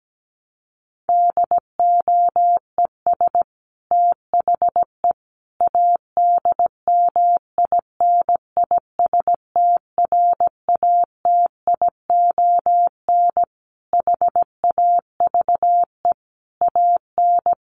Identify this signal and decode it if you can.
Morse code, heard as a keyed tone, about 17 wpm, 700 Hz. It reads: DOES THE ADMINISTRATION HAVE AN
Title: Sentences from Top 1300 Words 17wpm